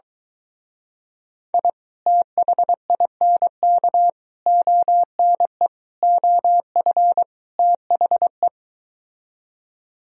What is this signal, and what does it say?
Morse code, heard as a keyed tone, about 23 wpm, 700 Hz. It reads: I THINK ONE OF THE